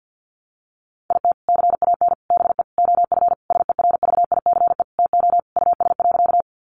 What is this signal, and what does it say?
Morse code, heard as a keyed tone, about 50 wpm, 700 Hz. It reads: IT CAN BE OF SERVICE TO US.
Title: Sentences from Top 1200 Words 50wpm